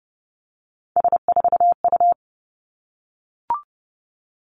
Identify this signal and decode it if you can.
Morse code, heard as a keyed tone, about 31 wpm, 700 Hz. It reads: S4U
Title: Three Challenging Letter-Number Combinations 30wpm